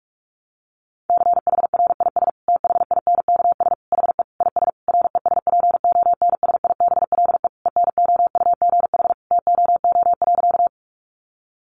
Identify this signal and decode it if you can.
Morse code, heard as a keyed tone, about 44 wpm, 700 Hz. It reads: CHRIS THINKS HE IS RESPONSIBLE ENOUGH TOO.
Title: Additional Easy Phrases 45wpm